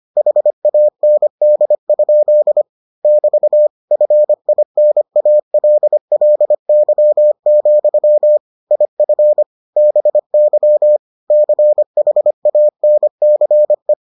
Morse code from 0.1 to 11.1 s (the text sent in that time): HAND? <BT> FINALLY, IF BY